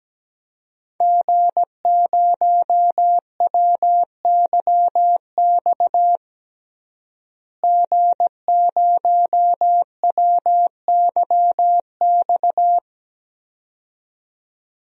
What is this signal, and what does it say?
Morse code, heard as a keyed tone, about 17 wpm, 700 Hz. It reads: G0WYX G0WYX
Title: DXCC-100 Call Signs - 5 Characters - Repeated 2x - 17wpm